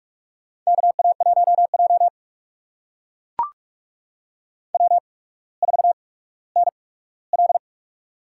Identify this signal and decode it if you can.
Morse code, heard as a keyed tone, about 44 wpm, spaced as for 19 wpm, 700 Hz. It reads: KA1J W4NL
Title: USA Call Signs - 4 Characters - 8x Character Spacing - 45wpm